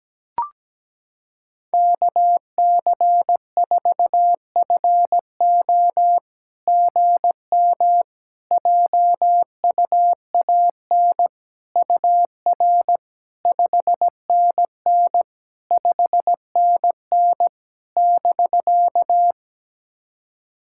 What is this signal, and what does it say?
Morse code, heard as a keyed tone, about 17 wpm, 690 Hz. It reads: KC4FO GM JUAN UR 5NN 5NN <BK>
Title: SOTA QSO Practice 17wpm